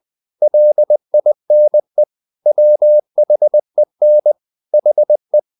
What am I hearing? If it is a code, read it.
Morse code, heard as a keyed tone, about 20 wpm, 600 Hz. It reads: LINE WHEN HE
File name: the old man and the sea / the old man and the sea part 1 20wpm